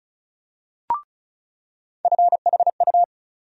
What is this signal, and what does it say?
Morse code, heard as a keyed tone, about 35 wpm, 700 Hz. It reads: FHU
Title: USA Call Sign Postfixes - 3 Letters 35wpm